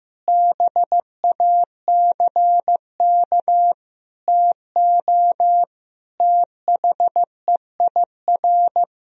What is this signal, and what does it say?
Morse code, heard as a keyed tone, about 15 wpm, 700 Hz. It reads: BACK TO THEIR
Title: Sentences from Top 1000 Words - Review Part 1 - 15wpm